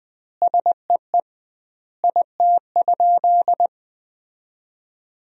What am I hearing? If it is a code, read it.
Morse code, heard as a keyed tone, about 21 wpm, 700 Hz. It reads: SEE IT?
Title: 2x Word-Spacing - Sentences from Top 100 Words - 20wpm